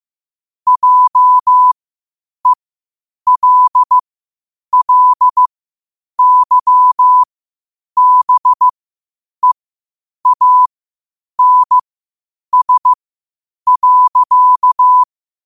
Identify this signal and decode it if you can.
Morse code, heard as a keyed tone, about 15 wpm, spaced as for 8 wpm, 1000 Hz. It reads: JELLYBEANS.
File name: qotdfiles / 8wpm